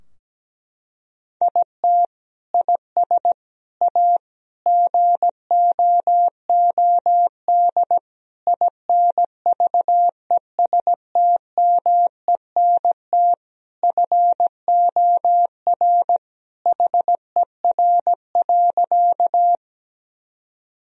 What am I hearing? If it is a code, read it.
Morse code, heard as a keyed tone, about 17 wpm, 700 Hz. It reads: IT IS A GOOD INVESTMENT FOR HER.
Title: Sentences from Top 1000 Words - Review Part 4 - 17wpm